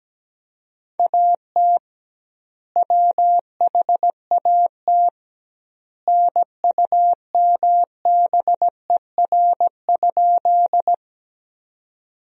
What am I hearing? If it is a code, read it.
Morse code, heard as a keyed tone, about 17 wpm, 700 Hz. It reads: AT WHAT NUMBER?
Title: 2x Word-Spacing - Sentences from Top 100 Words - 17wpm